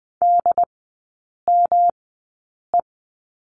Morse code, heard as a keyed tone, about 20 wpm, spaced as for 8 wpm, 700 Hz. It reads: DME
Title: 107 Lesson 18e - Letter m - Three Characters 20wpm